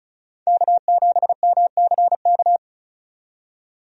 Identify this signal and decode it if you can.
Morse code, heard as a keyed tone, about 36 wpm, 700 Hz. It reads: K7MCK
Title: USA Call Signs - 5 Characters - 35wpm